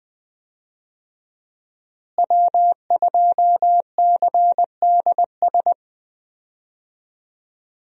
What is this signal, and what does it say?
Morse code, heard as a keyed tone, about 20 wpm, 700 Hz. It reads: W2CDS
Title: USA Call Signs - 5 Characters - Repeated 3x - 20wpm